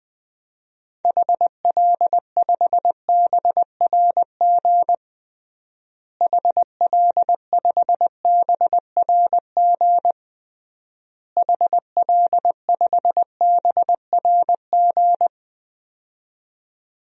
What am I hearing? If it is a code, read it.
Morse code, heard as a keyed tone, about 20 wpm, 700 Hz. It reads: HL5BRG HL5BRG HL5BRG
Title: DXCC-100 Call Signs - 6 Characters - Repeated 3x - 20wpm